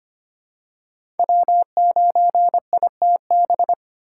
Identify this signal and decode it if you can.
Morse code, heard as a keyed tone, about 25 wpm, 700 Hz. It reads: W9ITB
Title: USA Call Signs - 4x Character Spacing - 25wpm